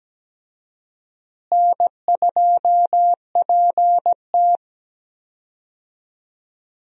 Morse code, heard as a keyed tone, about 17 wpm, 700 Hz. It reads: N2PT
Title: USA Call Signs - 4 Characters - Repeated 2x - 17wpm